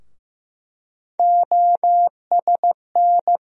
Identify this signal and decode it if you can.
Morse code, heard as a keyed tone, about 15 wpm, 700 Hz. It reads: OSN
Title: USA Call Sign Postfixes - 3 Letters 15wpm